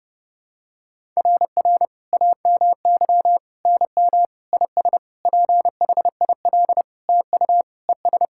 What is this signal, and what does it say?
Morse code, heard as a keyed tone, about 32 wpm, 700 Hz. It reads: RR AMY NM IS PHIL TU ES
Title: SOTA QSO Practice 30wpm